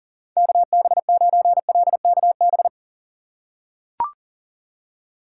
Morse code, heard as a keyed tone, about 40 wpm, 700 Hz. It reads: KB9LKB
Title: USA Call Signs - 6 Characters - 2x Character Spacing - 40wpm